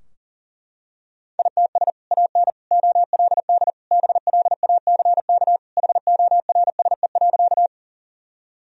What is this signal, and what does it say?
Morse code, heard as a keyed tone, about 40 wpm, 700 Hz. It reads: ITS AN OLD BLACK HORSE.
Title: Sentences from Top 1000 Words - Review Part 2 - 40wpm